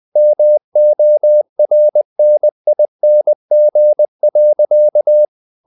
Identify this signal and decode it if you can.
Morse code, heard as a keyed tone, about 20 wpm, 600 Hz. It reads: MORNING.
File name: the old man and the sea / the old man and the sea part 1 20wpm